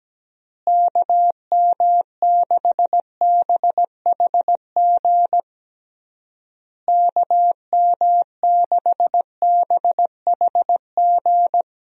Morse code, heard as a keyed tone, about 17 wpm, 700 Hz. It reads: KM6BHG KM6BHG
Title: USA Call Signs - 6 Characters - Repeated 3x - 17wpm